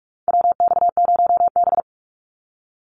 Morse code, heard as a keyed tone, about 44 wpm, 700 Hz. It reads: WX0B
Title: Call Signs Repeated 3x - 45wpm